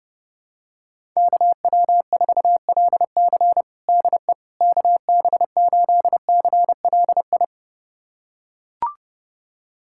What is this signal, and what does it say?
Morse code, heard as a keyed tone, about 30 wpm, 700 Hz. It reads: KW4LC DE KB8CLI
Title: USA Callsign DE Callsign 30wpm